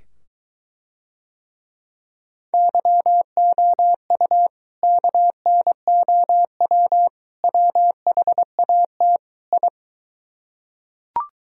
Morse code, heard as a keyed tone, about 23 wpm, 700 Hz. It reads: YOU KNOW WHAT I